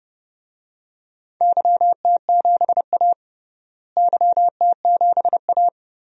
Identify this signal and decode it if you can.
Morse code, heard as a keyed tone, about 30 wpm, 700 Hz. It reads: YT7A YT7A
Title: DXCC-100 Call Signs - 4 Characters - Repeated 3x - 30wpm